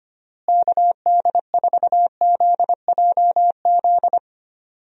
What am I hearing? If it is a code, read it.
Morse code, heard as a keyed tone, about 25 wpm, 700 Hz. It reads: KD4ZJZ
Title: USA Call Signs - 6 Characters - Repeated 2x - 25wpm